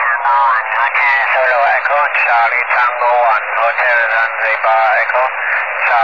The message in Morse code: none